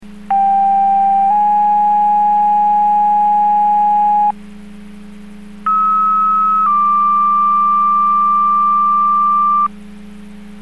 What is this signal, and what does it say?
Two-tone page, heard as a one-second tone, then a three-second tone, about 790 then 830 Hz, about 1290 then 1180 Hz. It is heard from 0.3 to 9.7 s.